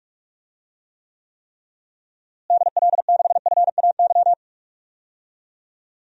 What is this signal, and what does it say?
Morse code, heard as a keyed tone, about 47 wpm, 700 Hz. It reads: DL6FAY
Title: DXCC-100 Call Signs - 6 Characters - Repeated 3x - 45wpm